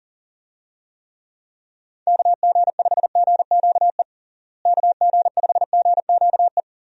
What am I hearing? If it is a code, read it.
Morse code, heard as a keyed tone, about 40 wpm, 700 Hz. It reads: KG5GQE KG5GQE
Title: USA Call Signs - 6 Characters - Repeated 3x - 40wpm